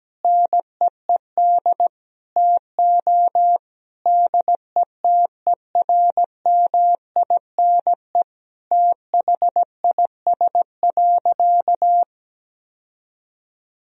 Morse code, heard as a keyed tone, about 17 wpm, 700 Hz. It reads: NEED TO DETERMINE THIS.
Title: Sentences from Top 1000 Words - Review Part 4 - 17wpm